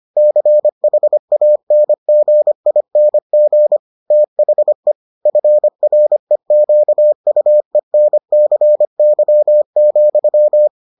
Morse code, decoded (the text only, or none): CHANGING THE FREQUENCY,